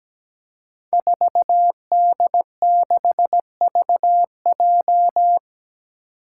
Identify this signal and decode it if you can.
Morse code, heard as a keyed tone, about 17 wpm, 700 Hz. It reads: 4D6VJ
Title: Five Challenging Letter-Number Combinations 17wpm